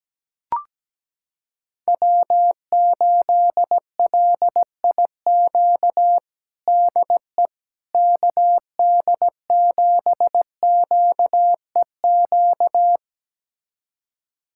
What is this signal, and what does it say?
Morse code, heard as a keyed tone, about 17 wpm, 700 Hz. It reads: W8LIQ DE KD7QEQ
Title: USA Callsign DE Callsign 17wpm